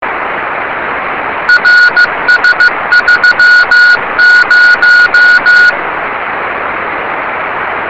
Morse code, heard as a keyed tone, about 15 wpm, 1450 Hz. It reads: RS30
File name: yubileiry cw by dk3wn